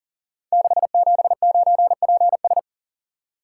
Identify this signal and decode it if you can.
Morse code, heard as a keyed tone, about 42 wpm, 700 Hz. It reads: 679PS
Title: Code Groups - 5 Letter:Numbers - 40wpm